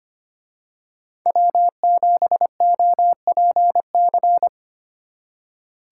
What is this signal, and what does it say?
Morse code, heard as a keyed tone, about 29 wpm, 700 Hz. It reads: W7OPC
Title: Speed Racing - USA 1x3 Call Signs 25wpm